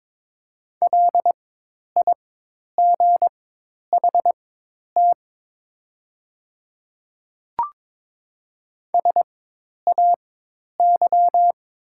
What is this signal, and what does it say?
Morse code, heard as a keyed tone, about 22 wpm, spaced as for 10 wpm, 700 Hz. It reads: LIGHT SAY
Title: 4x Character Spacing - Top 300 Words - 22wpm